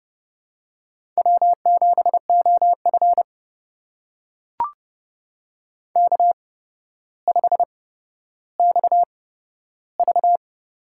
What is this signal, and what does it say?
Morse code, heard as a keyed tone, about 30 wpm, spaced as for 13 wpm, 700 Hz. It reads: W7OF K5XV
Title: USA Call Signs - 4 Characters - 8x Character Spacing - 30wpm